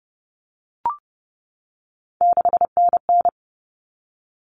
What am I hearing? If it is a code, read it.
Morse code, heard as a keyed tone, about 30 wpm, 700 Hz. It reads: 6NN
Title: Common Cut Numbers - 3-Digit 30wpm